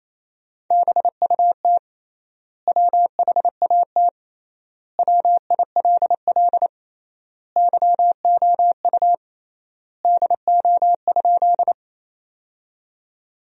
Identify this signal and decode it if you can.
Morse code, heard as a keyed tone, about 28 wpm, 700 Hz. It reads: BUT WHAT WILL YOU DO?
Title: 3x Word-Spacing - Sentences from Top 100 Words - 28wpm